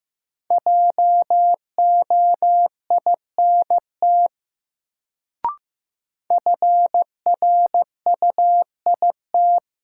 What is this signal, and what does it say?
Morse code, heard as a keyed tone, about 15 wpm, 700 Hz. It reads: JOINT FRUIT
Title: Sentences from Top 1300 Words - New Word Drill - 15wpm